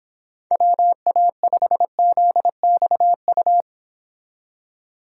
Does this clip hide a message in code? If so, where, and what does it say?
Yes: Morse code from 0.5 to 3.6 s, heard as a keyed tone, about 26 wpm, 700 Hz. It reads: WA5ZXU